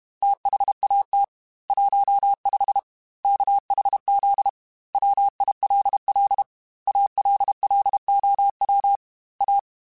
Morse code, heard as a keyed tone, about 32 wpm, 800 Hz. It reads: THAT 15 KHZ WILL ALLOW A
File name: Technical Subjects / Technical Subjects-mp3s / Sw Pwr Sup @30